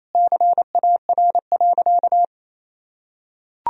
Morse code, heard as a keyed tone, about 28 wpm, 700 Hz. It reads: CAR.
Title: Sentences from Top 1000 Words - Review Part 1 - 28wpm